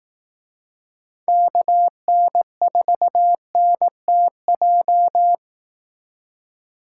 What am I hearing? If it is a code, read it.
Morse code, heard as a keyed tone, about 18 wpm, 700 Hz. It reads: KN4NTJ
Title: Speed Racing - USA 2x3 Call Signs 15wpm